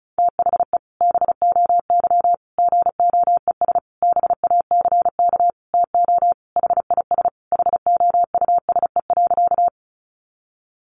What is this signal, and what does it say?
Morse code, heard as a keyed tone, about 34 wpm, 700 Hz. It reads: THE BOY GOES BACK TO HIS HOUSE.
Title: Additional Easy Phrases 35wpm